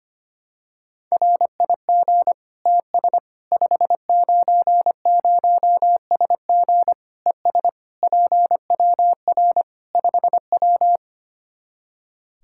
Morse code, heard as a keyed tone, about 25 wpm, 700 Hz. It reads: RIG TS 590SG ES PWR 5W
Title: Practice QSO - Exchange 2 - Info - 25wpm